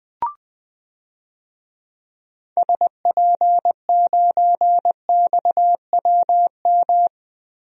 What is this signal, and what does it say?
Morse code, heard as a keyed tone, about 20 wpm, 700 Hz. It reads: SP9XWM
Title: DXCC-100 Call Signs - 6 Characters - Repeated 3x - 20wpm